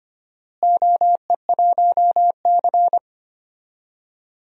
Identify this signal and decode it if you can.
Morse code, heard as a keyed tone, about 25 wpm, 700 Hz. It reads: OE1C
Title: DXCC-100 Call Signs - 4 Characters - Repeated 2x - 25wpm